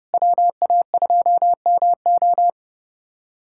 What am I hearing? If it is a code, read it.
Morse code, heard as a keyed tone, about 32 wpm, 700 Hz. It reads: WA2MO
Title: USA Callsign DE Callsign - 5 Characters - 30wpm